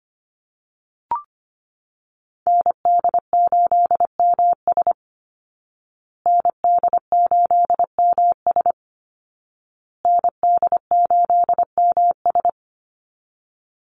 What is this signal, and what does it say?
Morse code, heard as a keyed tone, about 25 wpm, 700 Hz. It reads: ND8MS ND8MS ND8MS
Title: Call Signs Repeated 2x - 25wpm